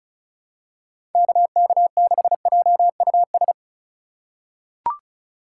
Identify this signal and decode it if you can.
Morse code, heard as a keyed tone, about 34 wpm, 700 Hz. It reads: KK6JUS
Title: USA Call Signs - 2x Character Spacing - 35wpm